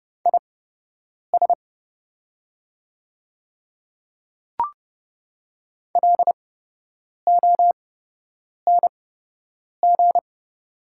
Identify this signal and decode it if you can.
Morse code, heard as a keyed tone, about 30 wpm, spaced as for 8 wpm, 700 Hz. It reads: IS LONG
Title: 8x Character Spacing - Top 100 Words - 30wpm